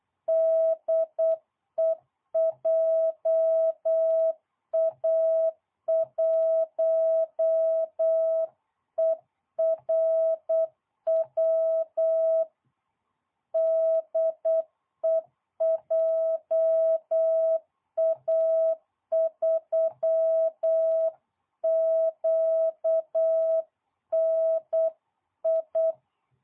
Morse code, heard as a keyed tone, about 8 wpm, 660 Hz. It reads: DE JA1ERW DE JA3QNI